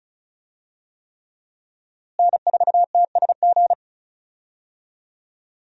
Morse code, heard as a keyed tone, about 36 wpm, 700 Hz. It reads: N4TSG
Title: USA Call Signs - 5 Characters - Repeated 3x - 35wpm